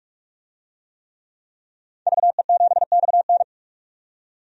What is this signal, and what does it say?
Morse code, heard as a keyed tone, about 46 wpm, 700 Hz. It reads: VE7XN
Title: DXCC-100 Call Signs - 5 Characters - 45wpm